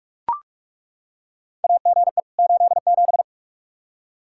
Morse code, heard as a keyed tone, about 41 wpm, 700 Hz. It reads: AGE 87
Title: K1USN SST QSO Practice 45wpm